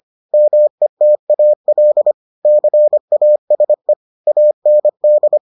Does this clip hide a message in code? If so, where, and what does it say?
Yes: Morse code from 0.3 to 5.4 s, heard as a keyed tone, about 25 wpm, 600 Hz. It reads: METAL CASE AND